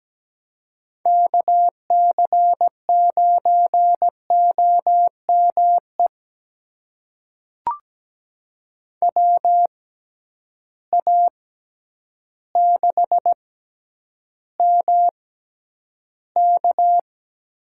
Morse code, heard as a keyed tone, about 17 wpm, spaced as for 9 wpm, 700 Hz. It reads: KC9OME WA6MK
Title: USA Call Signs - 6 Characters - 6x Character Spacing - 17wpm